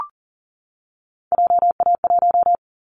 Morse code, heard as a keyed tone, about 40 wpm, 700 Hz. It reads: JA1
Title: Callsign Prefixes for Top 25 DXCC - 40wpm